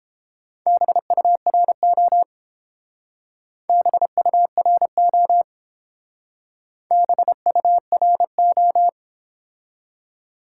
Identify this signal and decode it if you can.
Morse code, heard as a keyed tone, about 29 wpm, 700 Hz. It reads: BURO BURO BURO